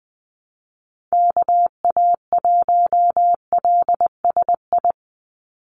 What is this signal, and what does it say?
Morse code, heard as a keyed tone, about 19 wpm, 700 Hz. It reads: KA1LSI
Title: USA Call Signs 20wpm